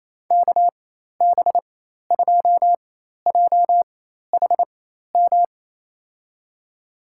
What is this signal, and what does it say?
Morse code, heard as a keyed tone, about 28 wpm, spaced as for 13 wpm, 700 Hz. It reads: KB2JHM
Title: USA Call Signs - 6 Characters - 4x Character Spacing - 28wpm